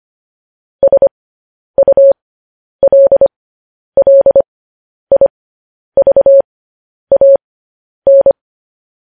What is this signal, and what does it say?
Morse code, heard as a keyed tone, about 25 wpm, spaced as for 10 wpm, 550 Hz. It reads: SULLIVAN
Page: news-headlines-in-morse-code-at-10-wpm